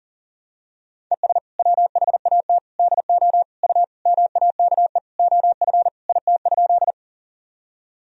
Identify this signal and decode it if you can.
Morse code, heard as a keyed tone, about 37 wpm, 700 Hz. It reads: ES WHAT DO U MAKE OF IT?
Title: Sentences from Top 100 Words and Abbreviations 40wpm